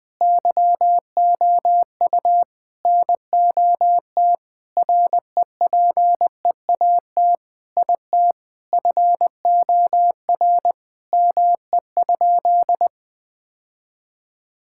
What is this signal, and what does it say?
Morse code, heard as a keyed tone, about 21 wpm, 700 Hz. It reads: YOU NOT REPEAT IT FOR ME?
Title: Sentences from Top 1600 Words 20wpm